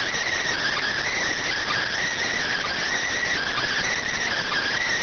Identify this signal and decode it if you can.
Emergency Alert System (SAME) tones, heard as harsh buzzing bursts. It is heard from 0.9 to 4.1 s.